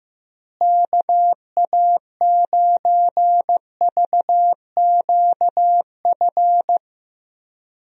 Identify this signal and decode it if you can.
Morse code, heard as a keyed tone, about 15 wpm, 700 Hz. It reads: KA9VQF
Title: USA Call Signs - 6 Characters - Repeated 2x - 15wpm